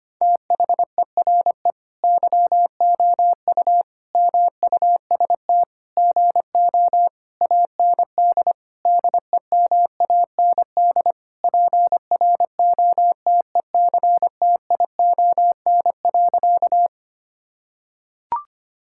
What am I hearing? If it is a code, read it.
Morse code, heard as a keyed tone, about 25 wpm, 700 Hz. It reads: THERE YOU MUST GO AND DEMAND PROTECTION.
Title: Sentences from Top 1600 Words 25wpm